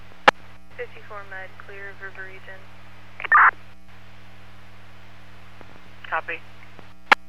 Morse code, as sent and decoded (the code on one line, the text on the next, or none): none